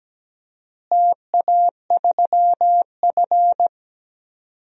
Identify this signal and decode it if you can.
Morse code, heard as a keyed tone, about 17 wpm, 700 Hz. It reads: TA3F
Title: DXCC-100 Call Signs - 4 Characters - Repeated 2x - 17wpm